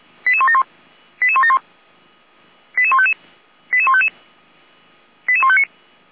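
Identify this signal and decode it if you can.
Five-tone selective call (ZVEI), heard as quick runs of five tones, tones of about 70 ms. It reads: 88171 88171 88155 88155 88159
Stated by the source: ZVEI five-tone selective call received on SDR, Feuerwehr Orlamünde.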